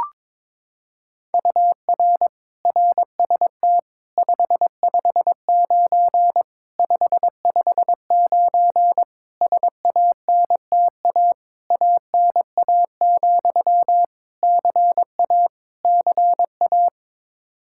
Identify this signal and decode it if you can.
Morse code, heard as a keyed tone, about 22 wpm, 700 Hz. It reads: UR RST 559 559 SANTA ANA, CA CA
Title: Practice QSO - Exchange 1 - Beginning - 22wpm